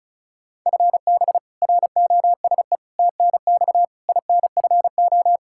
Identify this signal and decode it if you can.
Morse code, heard as a keyed tone, about 36 wpm, 700 Hz. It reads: FB ROSE TNX INFO
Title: Practice QSO - Exchange 3 - More Info - 35wpm